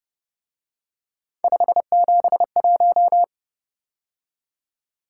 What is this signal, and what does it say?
Morse code, heard as a keyed tone, about 29 wpm, 700 Hz. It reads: 571
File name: Speed-Racing - Three Letter-Number Combinations - 25wpm